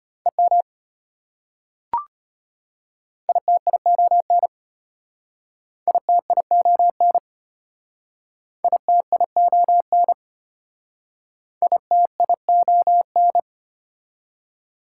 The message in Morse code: EM ITION ITION ITION ITION

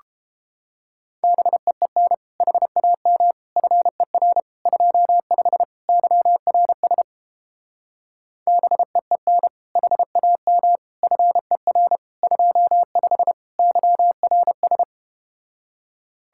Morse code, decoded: BEEN HAM FER 25 YRS BEEN HAM FER 25 YRS

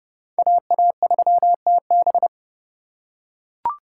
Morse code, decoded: AA3TB

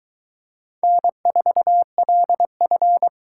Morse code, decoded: N4LF

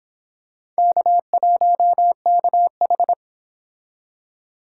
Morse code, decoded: K1KH

